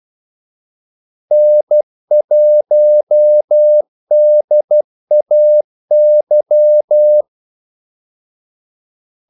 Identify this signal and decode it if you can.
Morse code, heard as a keyed tone, about 12 wpm, 600 Hz. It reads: N1DAY
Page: Kaldesignaler 12 wpm